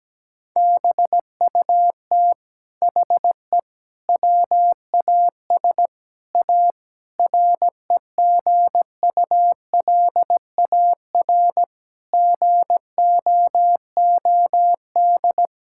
Morse code, heard as a keyed tone, about 17 wpm, 700 Hz. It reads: BUT HE WAS A REGULAR GOOD
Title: Sentences from Top 1000 Words - Review Part 6 - 17wpm